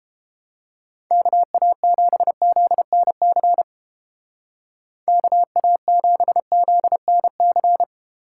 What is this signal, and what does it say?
Morse code, heard as a keyed tone, about 29 wpm, 700 Hz. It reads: KA7ZNC KA7ZNC